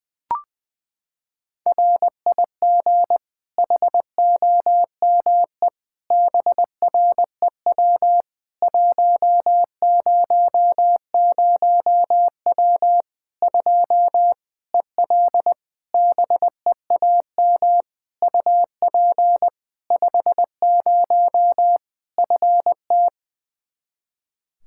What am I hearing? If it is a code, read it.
Morse code, heard as a keyed tone, about 20 wpm, 700 Hz. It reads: RIG HOME BREW 100W 2 EL BEAM UP 50 FT
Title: Practice QSO - Exchange 2 - Info - 20wpm